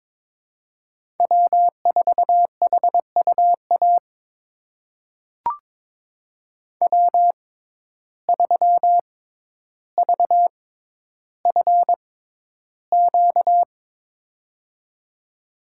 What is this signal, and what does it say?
Morse code, heard as a keyed tone, about 22 wpm, spaced as for 11 wpm, 700 Hz. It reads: W4HUA W3VFQ
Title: USA Call Signs - 6x Character Spacing - 22wpm